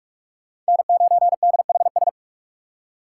Morse code, heard as a keyed tone, about 59 wpm, 700 Hz. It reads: N9DHS